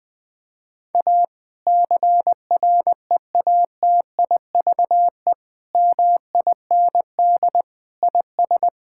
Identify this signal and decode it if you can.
Morse code, heard as a keyed tone, about 20 wpm, 700 Hz. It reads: A CREATIVE MIND IS
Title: Sentences from Top 1000 Words - Review Part 5 - 20wpm